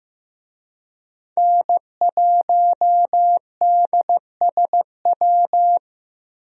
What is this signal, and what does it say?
Morse code, heard as a keyed tone, about 15 wpm, 700 Hz. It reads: N1DSW
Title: USA Call Signs - 5 Characters - Repeated 3x - 15wpm